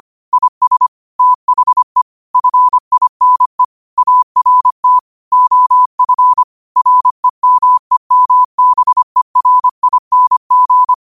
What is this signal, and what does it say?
Morse code, heard as a keyed tone, about 25 wpm, 1000 Hz. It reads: IS THE FINE ART OF REMEMBERING
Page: New quotes every day in morse code at 25 Words per minute.